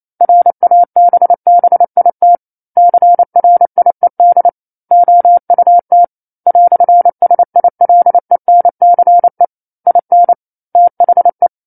Morse code, heard as a keyed tone, about 28 wpm, 700 Hz. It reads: RABBIT CRIED OUT "SILENCE IN THE
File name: 09-Chapter-09 28 WPM